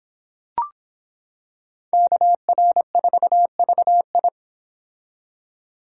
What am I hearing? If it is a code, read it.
Morse code, heard as a keyed tone, about 26 wpm, 700 Hz. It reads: KR4VI